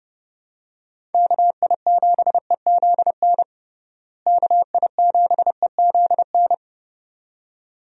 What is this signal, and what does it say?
Morse code, heard as a keyed tone, about 32 wpm, 700 Hz. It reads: KI7EZN KI7EZN
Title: USA Call Signs - 6 Characters - Repeated 2x - 30wpm